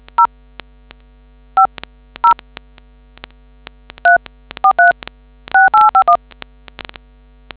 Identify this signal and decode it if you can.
DTMF, heard as telephone touch tones, tones of about 110 ms. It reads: 0 50 3436851